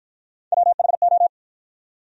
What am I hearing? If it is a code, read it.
Morse code, heard as a keyed tone, about 52 wpm, 700 Hz. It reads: WHO